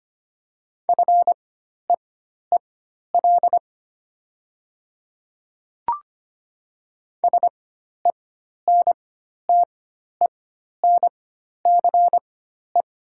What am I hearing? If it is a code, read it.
Morse code, heard as a keyed tone, about 25 wpm, spaced as for 12 wpm, 700 Hz. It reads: FEEL SENTENCE
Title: 4x Character Spacing - Top 500 Words - 25wpm